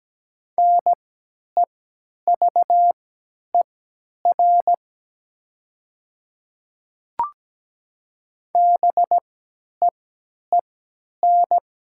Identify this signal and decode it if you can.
Morse code, heard as a keyed tone, about 17 wpm, spaced as for 10 wpm, 700 Hz. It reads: NEVER BEEN
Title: 3x Character Spacing - Top 300 Words - 17wpm